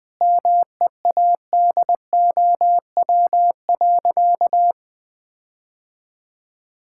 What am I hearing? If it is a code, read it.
Morse code, heard as a keyed tone, about 21 wpm, 700 Hz. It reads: MEADOW.
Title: Difficult Sentences with Hints - 20wpm